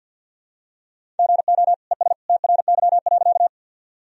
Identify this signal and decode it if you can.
Morse code, heard as a keyed tone, about 51 wpm, 700 Hz. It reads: GO ES TRY.
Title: Sentences from Top 200 Words and Abbreviations 50wpm